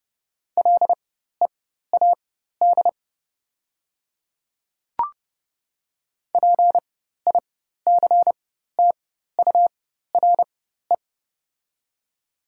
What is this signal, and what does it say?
Morse code, heard as a keyed tone, about 30 wpm, spaced as for 14 wpm, 690 Hz. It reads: LEAD PICTURE
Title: 4x Character Spacing - Top 500 Words - 30wpm